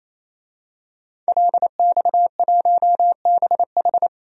Morse code, heard as a keyed tone, about 28 wpm, 700 Hz. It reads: LX1BH
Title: DXCC-100 Call Signs - 5 Characters - 28wpm